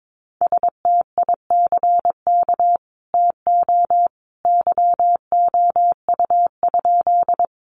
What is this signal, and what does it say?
Morse code, heard as a keyed tone, about 22 wpm, 700 Hz. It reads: STICK TO YOU?
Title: Sentences from Top 1200 Words 22wpm